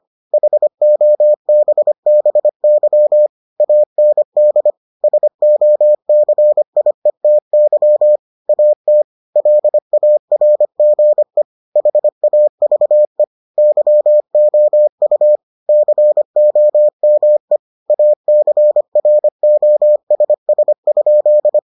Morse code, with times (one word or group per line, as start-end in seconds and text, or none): 0.3-3.3 HOBBY
3.6-4.7 AND
5.0-8.2 SOCIETY
8.5-9.0 AT
9.4-11.4 LARGE
11.8-13.2 HAVE
13.6-15.4 YOU
15.7-17.6 COME
17.9-21.6 ACROSS?